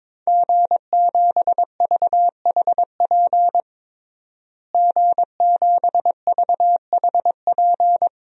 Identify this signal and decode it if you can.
Morse code, heard as a keyed tone, about 22 wpm, 700 Hz. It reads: G7VHP G7VHP
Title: DXCC-100 Call Signs - 5 Characters - Repeated 2x - 22wpm